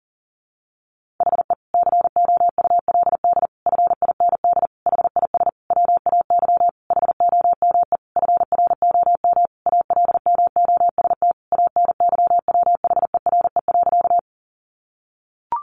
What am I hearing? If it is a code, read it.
Morse code, heard as a keyed tone, about 40 wpm, 700 Hz. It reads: HE COULD FIND HIS WAY HOME FROM ALMOST ANYWHERE.
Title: Sentences from Top 1400 Words 40wpm